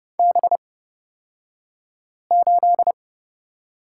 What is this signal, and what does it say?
Morse code, heard as a keyed tone, about 30 wpm, spaced as for 5 wpm, 700 Hz. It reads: B8
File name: Lesson 36d - BT - Two Characters 30wpm